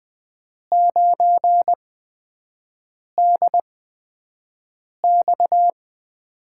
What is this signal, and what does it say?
Morse code, heard as a keyed tone, about 20 wpm, spaced as for 5 wpm, 700 Hz. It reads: 9DX
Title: USA Call Signs - 5 Characters - 8x Character Spacing - 20wpm